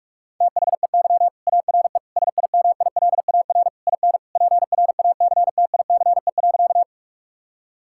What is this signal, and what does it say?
Morse code, heard as a keyed tone, about 41 wpm, 700 Hz. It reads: THEY ARE SIMILAR IN PRACTICE.
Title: Sentences from Top 1000 Words - Review Part 5 - 45wpm